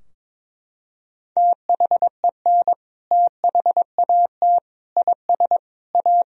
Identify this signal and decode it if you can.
Morse code, heard as a keyed tone, about 22 wpm, 700 Hz. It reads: THEN THAT IS A
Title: Sentences from Top 1000 Words - Review Part 4 - 22wpm